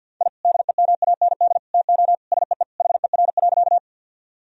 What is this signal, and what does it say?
Morse code, heard as a keyed tone, about 50 wpm, 700 Hz. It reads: I DEMAND TO SEE HER.
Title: Sentences from Top 1000 Words - Review Part 4 - 50wpm